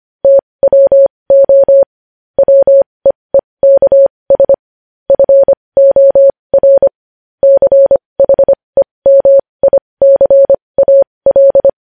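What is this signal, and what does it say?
Morse code, heard as a keyed tone, about 25 wpm, spaced as for 20 wpm, 550 Hz. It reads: TWO WEEKS FOR CHEMICAL
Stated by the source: Morse code news headlines at 20 wpm.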